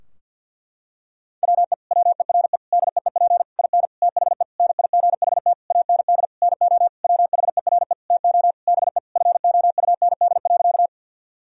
Morse code, heard as a keyed tone, about 52 wpm, 700 Hz. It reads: WE WERE DEEP IN THE NIGHT AND NO WHERE TO BE FOUND.
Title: Sentences from Top 1000 Words - Review Part 3 - 50wpm